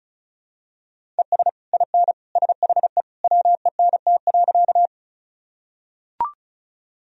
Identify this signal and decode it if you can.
Morse code, heard as a keyed tone, about 36 wpm, 700 Hz. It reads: ES IN SHE WENT.
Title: Sentences from Top 200 Words and Abbreviations 35wpm